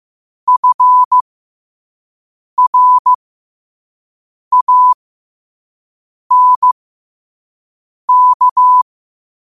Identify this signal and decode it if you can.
Morse code, heard as a keyed tone, about 15 wpm, spaced as for 5 wpm, 1000 Hz. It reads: FRANK